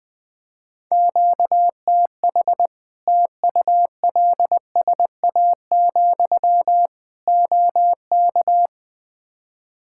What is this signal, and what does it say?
Morse code, heard as a keyed tone, about 20 wpm, 700 Hz. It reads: QTH TULSA, OK
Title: Practice QSO - Exchange 1 - Beginning - 20wpm